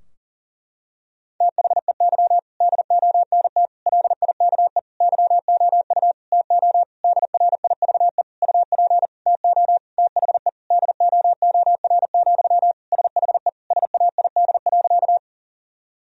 Morse code, heard as a keyed tone, about 40 wpm, 700 Hz. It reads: THEY DONT LIKE YOU TO DRIVE UP TO THE DOOR, SHE SAID.
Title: Sentences from Top 1000 Words - Review Part 2 - 40wpm